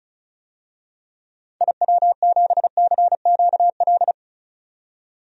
Morse code, heard as a keyed tone, about 34 wpm, 700 Hz. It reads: IW7CQL
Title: DXCC-100 Call Signs - 6 Characters - 35wpm